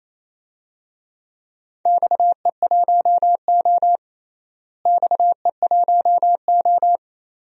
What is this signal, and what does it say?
Morse code, heard as a keyed tone, about 28 wpm, 700 Hz. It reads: XE1O XE1O
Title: DXCC-100 Call Signs - 4 Characters - Repeated 2x - 28wpm